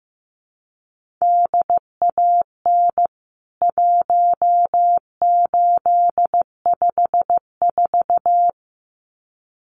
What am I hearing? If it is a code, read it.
Morse code, heard as a keyed tone, about 15 wpm, 700 Hz. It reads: DAN 1854
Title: CWT Exchange 15wpm